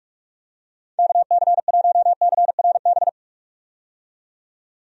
Morse code, heard as a keyed tone, about 46 wpm, 700 Hz. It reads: KC1CRB
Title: USA Call Signs - 6 Characters - 45wpm